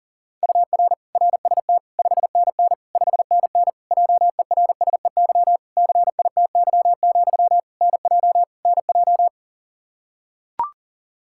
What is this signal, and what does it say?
Morse code, heard as a keyed tone, about 37 wpm, 700 Hz. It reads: UR RST 5NN 5NN JERSEY CITY, NJ NJ
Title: Practice QSO - Exchange 1 - Beginning - 40wpm